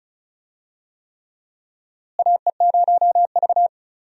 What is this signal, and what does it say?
Morse code, heard as a keyed tone, about 36 wpm, 700 Hz. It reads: AE0V
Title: USA Call Signs - 4 Characters - Repeated 3x - 35wpm